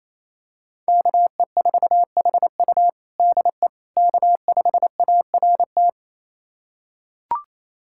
Morse code, heard as a keyed tone, about 28 wpm, 700 Hz. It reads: KE4HU DE K5ART
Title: USA Callsign DE Callsign - 5 Characters - 28wpm